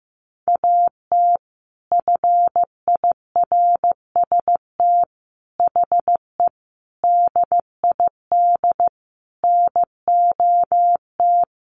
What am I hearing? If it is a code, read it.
Morse code, heard as a keyed tone, about 15 wpm, 700 Hz. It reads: AT FIRST HE DID NOT
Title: Sentences from Top 600 Words 15wpm